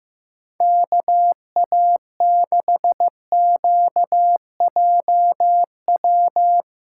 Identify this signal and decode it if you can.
Morse code, heard as a keyed tone, about 15 wpm, 700 Hz. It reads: KA6QJW
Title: USA Call Signs - 3x Character Spacing - 15wpm